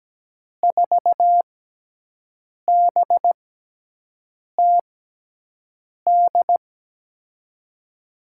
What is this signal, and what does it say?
Morse code, heard as a keyed tone, about 17 wpm, spaced as for 6 wpm, 700 Hz. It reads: 4BTD
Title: USA Call Signs - 6 Characters - 6x Character Spacing - 17wpm